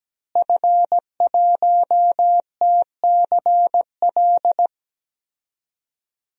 Morse code, heard as a keyed tone, about 17 wpm, 700 Hz. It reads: F1TCL
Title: DXCC-100 Call Signs - 5 Characters - Repeated 3x - 17wpm